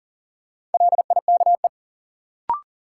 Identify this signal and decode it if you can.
Morse code, heard as a keyed tone, about 40 wpm, 700 Hz. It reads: LIKE
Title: Sentences from Top 100 Words - New Word Drill - 40wpm